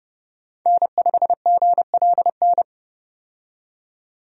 Morse code, heard as a keyed tone, about 29 wpm, 700 Hz. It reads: N5GLN
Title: USA Call Signs - 5 Characters - 30wpm